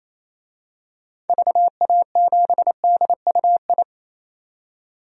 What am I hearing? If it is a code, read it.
Morse code, heard as a keyed tone, about 28 wpm, 700 Hz. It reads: VA7DUI
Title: DXCC-100 Call Signs - 6 Characters - 28wpm